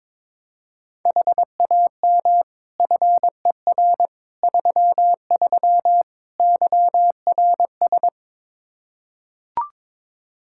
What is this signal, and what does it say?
Morse code, heard as a keyed tone, about 22 wpm, 700 Hz. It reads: HAM FER 33 YRS
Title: Practice QSO - Exchange 3 - More Info - 22wpm